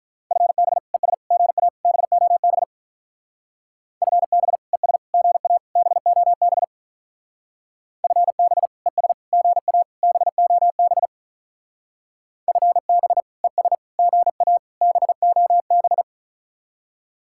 Morse code, heard as a keyed tone, about 45 wpm, 700 Hz. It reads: FB ES GA BOB FB ES GA BOB FB ES GA BOB FB ES GA BOB